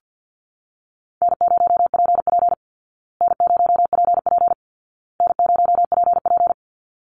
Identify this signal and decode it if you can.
Morse code, heard as a keyed tone, about 50 wpm, 700 Hz. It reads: N0PP N0PP N0PP
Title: Easy Call Signs Repeated 3x - 50wpm